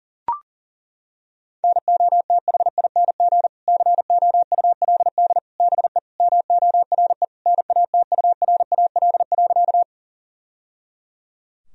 Morse code, heard as a keyed tone, about 40 wpm, 700 Hz. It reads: NOTHING COULD BE MORE NATURAL.
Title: Sentences from Top 1000 Words - Review Part 4 - 40wpm